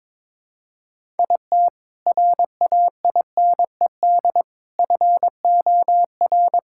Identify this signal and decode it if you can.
Morse code, heard as a keyed tone, about 22 wpm, 700 Hz. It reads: IT RAINED FOR
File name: Sentences with 7-Word Phrases - 22wpm